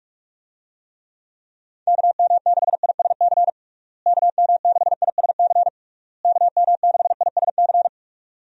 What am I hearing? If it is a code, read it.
Morse code, heard as a keyed tone, about 46 wpm, 700 Hz. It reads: KM6ISC KM6ISC KM6ISC
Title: USA Call Signs - 6 Characters - Repeated 3x - 45wpm